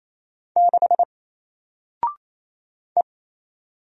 Morse code, heard as a keyed tone, about 28 wpm, 700 Hz. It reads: 6 E
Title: Single Letter-Number Alt 28wpm